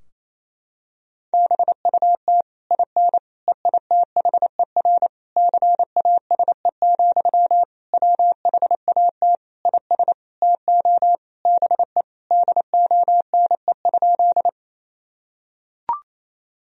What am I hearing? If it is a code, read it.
Morse code, heard as a keyed tone, about 28 wpm, 700 Hz. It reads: BUT IN EITHER CASE, WHAT IS TO BE DONE?
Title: Sentences from Top 1000 Words - Review Part 4 - 28wpm